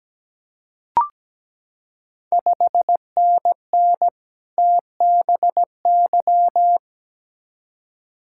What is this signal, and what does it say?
Morse code, heard as a keyed tone, about 17 wpm, 700 Hz. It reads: 5NN TBY
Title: Ontario QSO Party Exchange 17wpm